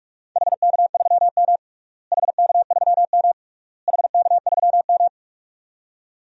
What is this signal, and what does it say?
Morse code, heard as a keyed tone, about 46 wpm, 700 Hz. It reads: HK3M HK3M HK3M
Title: DXCC-100 Call Signs - 4 Characters - Repeated 3x - 45wpm